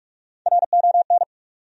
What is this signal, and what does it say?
Morse code, heard as a keyed tone, about 46 wpm, 700 Hz. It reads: RON
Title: USA Common Names - Max 5 Letters - 45wpm